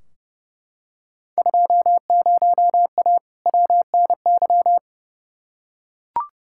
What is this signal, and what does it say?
Morse code, heard as a keyed tone, about 30 wpm, 700 Hz. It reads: 20A WNY
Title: Field Day Exchanges Encore - 30wpm